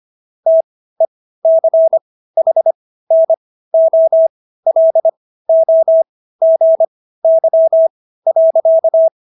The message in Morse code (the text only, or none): TECHNOLOGY.